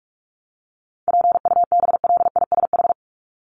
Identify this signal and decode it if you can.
Morse code, heard as a keyed tone, about 44 wpm, 700 Hz. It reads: PUBLISH
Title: ED Suffix - Top 100 Words with Base Words 45wpm